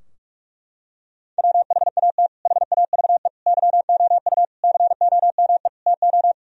Morse code, heard as a keyed tone, about 46 wpm, 700 Hz. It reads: WHAT HAVE YOU COME TO
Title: Sentences from Top 1000 Words - Review Part 2 - 45wpm